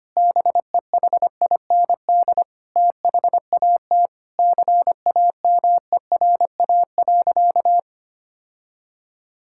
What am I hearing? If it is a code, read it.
Morse code, heard as a keyed tone, about 25 wpm, 700 Hz. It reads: BEHIND THAT CAMERA.
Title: Sentences from Top 1000 Words - Review Part 5 - 25wpm